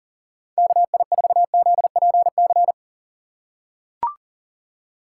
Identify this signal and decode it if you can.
Morse code, heard as a keyed tone, about 40 wpm, 700 Hz. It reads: KI4ZPC
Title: USA Call Signs - 2x Character Spacing - 40wpm